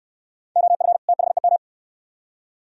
Morse code, heard as a keyed tone, about 59 wpm, 700 Hz. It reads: OF THEM